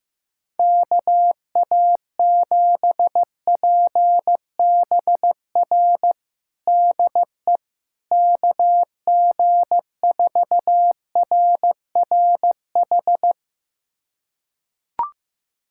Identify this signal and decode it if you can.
Morse code, heard as a keyed tone, about 15 wpm, 700 Hz. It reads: KA7PBR DE KG4RRH
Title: USA Callsign DE Callsign - 6 Characters - 15wpm